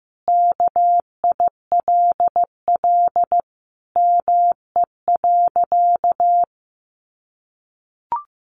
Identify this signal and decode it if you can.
Morse code, heard as a keyed tone, about 15 wpm, 700 Hz. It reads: KILL ME.
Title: Sentences from Top 1400 Words 15wpm